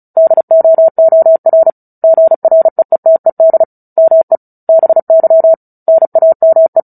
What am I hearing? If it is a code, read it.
Morse code, heard as a keyed tone, about 34 wpm, 650 Hz. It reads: DOOR GREETED ME BY NAME
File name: Book Two / 09-Chapter 9 - 35 WPM